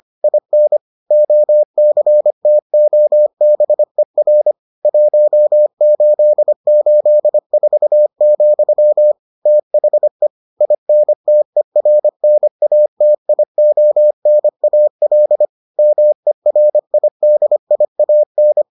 Morse code, heard as a keyed tone, about 25 wpm, 600 Hz. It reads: IN OCTOBER 1884, THE INTERNATIONAL MERIDIAN